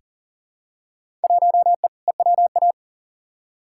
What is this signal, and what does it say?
Morse code, heard as a keyed tone, about 40 wpm, 700 Hz. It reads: 1E EWA
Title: Field Day Exchanges - 40wpm